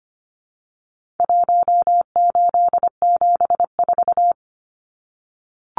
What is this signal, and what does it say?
Morse code, heard as a keyed tone, about 25 wpm, 700 Hz. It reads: 1874
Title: Numbers - Contest Style - 4-Digit 25wpm